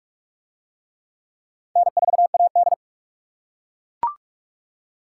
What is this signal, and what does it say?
Morse code, heard as a keyed tone, about 40 wpm, 700 Hz. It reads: N4AD